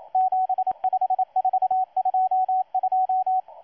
Morse code, heard as a keyed tone, about 27 wpm, 750 Hz, with radio noise. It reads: 75422
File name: M12-2016-07-07-1310utc-13926khz-error-restart